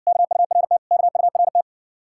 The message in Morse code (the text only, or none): KURT KURT